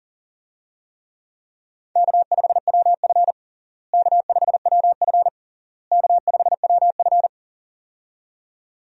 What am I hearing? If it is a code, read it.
Morse code, heard as a keyed tone, about 40 wpm, 700 Hz. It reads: K5WF K5WF K5WF
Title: USA Call Signs - 4 Characters - Repeated 3x - 40wpm